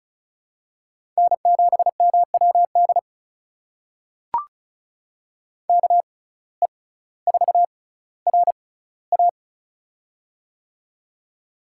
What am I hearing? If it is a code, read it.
Morse code, heard as a keyed tone, about 36 wpm, spaced as for 18 wpm, 700 Hz. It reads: N7MWD KE4RA
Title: USA Call Signs - 5 Characters - 6x Character Spacing - 35wpm